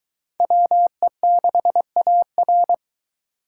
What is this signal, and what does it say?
Morse code, heard as a keyed tone, about 23 wpm, 700 Hz. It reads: WE6AR